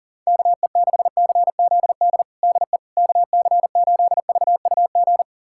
Morse code, heard as a keyed tone, about 40 wpm, 700 Hz. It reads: KE6CZD DE KC8VUG
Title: USA Callsign DE Callsign - 6 Characters - 40wpm